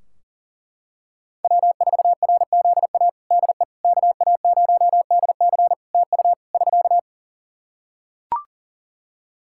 Morse code, heard as a keyed tone, about 40 wpm, 700 Hz. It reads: W4RZA DE KA0DC TU <SK>
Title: Practice QSO - Exchange 4 - Ending - 40wpm